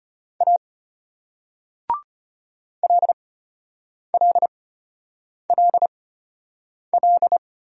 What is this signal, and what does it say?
Morse code, heard as a keyed tone, about 34 wpm, spaced as for 8 wpm, 700 Hz. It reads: A LLLL